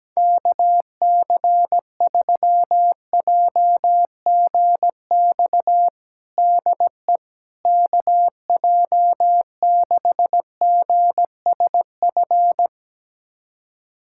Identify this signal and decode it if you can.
Morse code, heard as a keyed tone, about 17 wpm, 700 Hz. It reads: KC3JGX DE KJ6GSF
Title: USA Callsign DE Callsign - 6 Characters - 17wpm